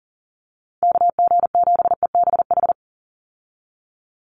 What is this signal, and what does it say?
Morse code, heard as a keyed tone, about 40 wpm, 700 Hz. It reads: KG7EBH
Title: USA Call Signs 40wpm